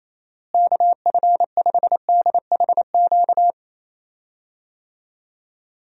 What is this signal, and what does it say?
Morse code, heard as a keyed tone, about 28 wpm, 700 Hz. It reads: KF5DHQ
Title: USA Call Signs - 6 Characters - Repeated 3x - 28wpm